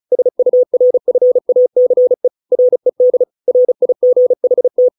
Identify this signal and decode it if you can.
Morse code, heard as a keyed tone, about 34 wpm, 500 Hz. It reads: SURFACE RED RIGHT